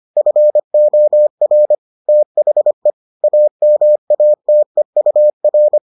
Morse code, heard as a keyed tone, about 25 wpm, 600 Hz. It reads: FOR THE AMATEUR